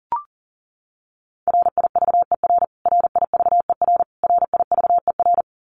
Morse code, heard as a keyed tone, about 40 wpm, 700 Hz. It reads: RIVER RIVER RIVER
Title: Top 300 Words Repeated 40wpm